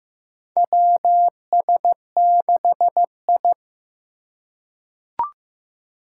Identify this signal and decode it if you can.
Morse code, heard as a keyed tone, about 16 wpm, 700 Hz. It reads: WS6I
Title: USA Call Signs - 4 Characters - 3x Character Spacing - 15wpm